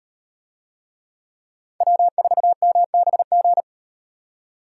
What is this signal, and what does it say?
Morse code, heard as a keyed tone, about 38 wpm, 700 Hz. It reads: W4MBG